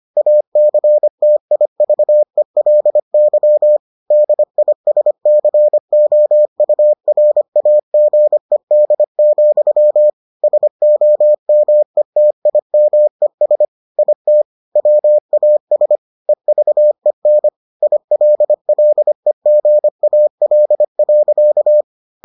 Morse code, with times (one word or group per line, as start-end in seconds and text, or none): 0.2-3.8 ACTIVELY
4.1-10.1 DISCOURAGED,
10.4-13.6 SOMETIMES
14.0-14.4 IT
14.8-16.0 WAS
16.3-17.5 EVEN
17.8-21.8 ILLEGAL.